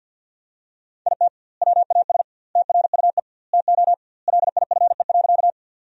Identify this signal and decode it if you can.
Morse code, heard as a keyed tone, about 50 wpm, 700 Hz. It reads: IT WAS TRUE TO LIFE.
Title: 2x Word-Spacing - Sentences from Top 400 Words - 50wpm